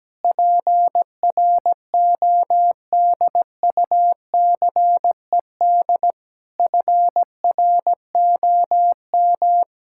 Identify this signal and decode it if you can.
Morse code, heard as a keyed tone, about 17 wpm, 700 Hz. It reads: PRODUCED FROM
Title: Sentences from Top 1000 Words - Review Part 6 - 17wpm